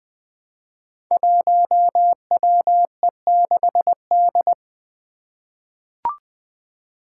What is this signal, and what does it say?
Morse code, heard as a keyed tone, about 20 wpm, 700 Hz. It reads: 1WE6D
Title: Five Challenging Letter-Number Combinations 20wpm